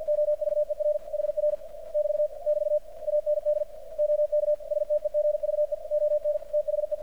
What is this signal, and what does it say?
Morse code, heard as a keyed tone, about 47 wpm, 600 Hz, with radio noise. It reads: OVER HR <BT> <BT> AND OK STEGVE ON THE
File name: archived / CRSnet